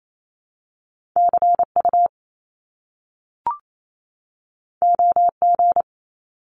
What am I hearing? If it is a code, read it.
Morse code, heard as a keyed tone, about 28 wpm, 700 Hz. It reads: CU OG
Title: Top Two Letter Combinations in Words 28wpm